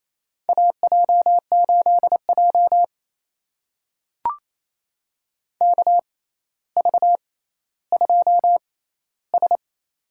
Morse code, heard as a keyed tone, about 28 wpm, spaced as for 14 wpm, 700 Hz. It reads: AJ8J KV2S
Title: USA Call Signs - 4 Characters - 6x Character Spacing - 28wpm